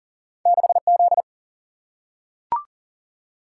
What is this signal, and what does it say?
Morse code, heard as a keyed tone, about 40 wpm, 700 Hz. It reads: 6 Z
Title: Single-Letter Single-Number 40wpm